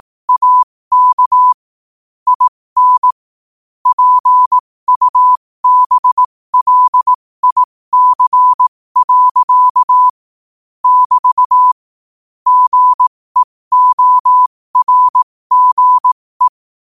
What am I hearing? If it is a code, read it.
Morse code, heard as a keyed tone, about 18 wpm, spaced as for 15 wpm, 1000 Hz. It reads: AK IN PUBLIC. <BT> GEORGE
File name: qotdfiles / 15wpm